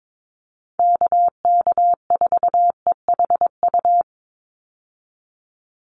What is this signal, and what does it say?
Morse code, heard as a keyed tone, about 22 wpm, 700 Hz. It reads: KK4EHU
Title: USA Call Signs 22wpm